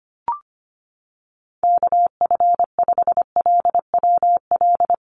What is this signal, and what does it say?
Morse code, heard as a keyed tone, about 25 wpm, 700 Hz. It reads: KF5LWL
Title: USA Call Signs Repeated 2x 25wpm